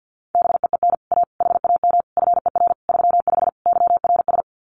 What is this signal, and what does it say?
Morse code, heard as a keyed tone, about 50 wpm, 700 Hz. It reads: BEEN A HAM FER 35 YRS
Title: Common QSO Phrases Encore 50wpm